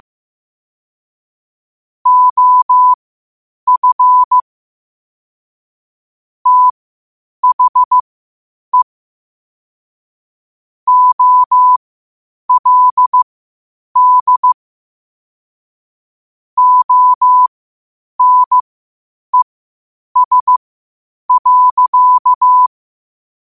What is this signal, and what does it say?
Morse code, heard as a keyed tone, about 15 wpm, spaced as for 8 wpm, 1000 Hz. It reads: OF THE OLD ONES.
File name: qotdfiles / 8wpm